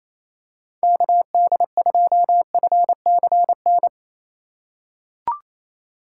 Morse code, heard as a keyed tone, about 28 wpm, 700 Hz. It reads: KD2FCN
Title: USA Call Signs - 6 Characters - 3x Character Spacing - 28wpm